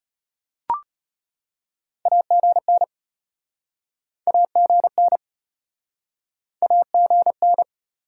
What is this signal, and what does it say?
Morse code, heard as a keyed tone, about 33 wpm, 700 Hz. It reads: AGN AGN AGN